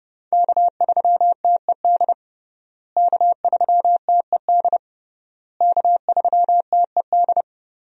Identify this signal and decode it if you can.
Morse code, heard as a keyed tone, about 30 wpm, 700 Hz. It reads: K3TED K3TED K3TED
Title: USA Call Signs - 5 Characters - Repeated 3x - 30wpm